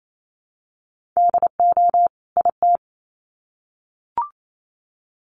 Morse code, heard as a keyed tone, about 28 wpm, 700 Hz. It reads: DO IT
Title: Sets of 2 Words 28wpm